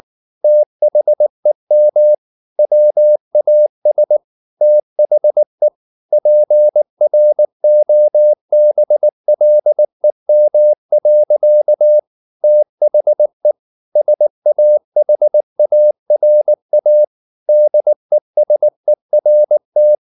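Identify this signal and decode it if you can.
Morse code, heard as a keyed tone, about 19 wpm, 600 Hz. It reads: THEM WAS THE PROBLEM. THE SAHARA DESERT
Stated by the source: Morse code at 19 wpm from CW med Gnister, reading Ken Follett's novel Never in English.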